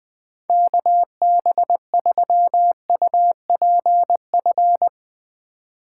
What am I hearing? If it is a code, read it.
Morse code, heard as a keyed tone, about 20 wpm, 700 Hz. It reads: KB3UPF
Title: USA Call Signs - 6 Characters - 2x Character Spacing - 20wpm